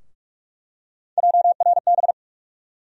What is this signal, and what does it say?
Morse code, heard as a keyed tone, about 46 wpm, 700 Hz. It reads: JRB
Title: USA Call Sign Postfixes - 3 Letters 45wpm